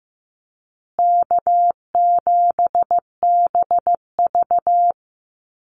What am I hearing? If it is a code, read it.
Morse code, heard as a keyed tone, about 15 wpm, 700 Hz. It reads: K7BV
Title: Easy Call Signs Repeated 3x - 15wpm